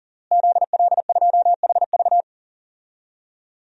Morse code, heard as a keyed tone, about 40 wpm, 700 Hz. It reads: ZL2HV
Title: DXCC-100 Call Signs - 5 Characters - Repeated 3x - 40wpm